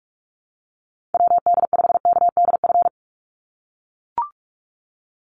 Morse code, heard as a keyed tone, about 44 wpm, 700 Hz. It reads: WD5KDF
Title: USA Call Signs 45wpm